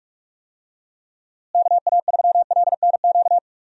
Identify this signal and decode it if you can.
Morse code, heard as a keyed tone, about 46 wpm, 700 Hz. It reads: KA3LNQ
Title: USA Call Signs - 6 Characters - Repeated 2x - 45wpm